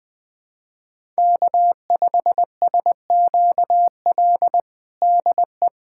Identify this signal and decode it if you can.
Morse code, heard as a keyed tone, about 20 wpm, 700 Hz. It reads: K5SQL DE
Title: USA Callsign DE Callsign 20wpm